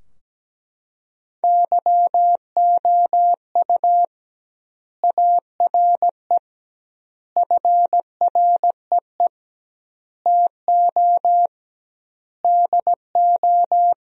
Morse code, heard as a keyed tone, about 17 wpm, 700 Hz. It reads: YOU ARE FREE TO DO
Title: 2x Word-Spacing - Sentences from Top 500 Words - 17wpm